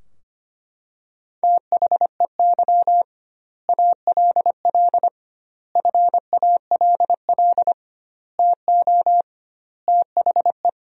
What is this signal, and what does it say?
Morse code, heard as a keyed tone, about 25 wpm, 700 Hz. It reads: THEY ALL FALL TO THE
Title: 2x Word-Spacing - Sentences from Top 500 Words - 25wpm